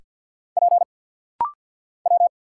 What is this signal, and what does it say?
Morse code, heard as a keyed tone, about 50 wpm, 700 Hz. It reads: P W
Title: Single Letters - Rapid-Fire 50wpm